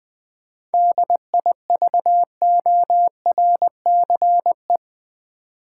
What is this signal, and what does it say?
Morse code, heard as a keyed tone, about 21 wpm, 700 Hz. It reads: DIVORCE
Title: 7 Letter Words - First 3 Letters Given - 20wpm